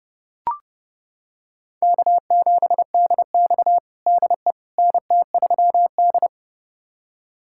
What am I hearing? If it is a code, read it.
Morse code, heard as a keyed tone, about 30 wpm, 700 Hz. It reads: K7DX DE NT3D
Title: USA Callsign DE Callsign - 4 Characters - 30wpm